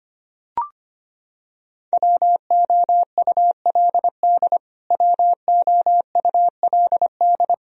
Morse code, heard as a keyed tone, about 25 wpm, 700 Hz. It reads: WOULD WOULD
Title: Top 100 Words Repeated 25wpm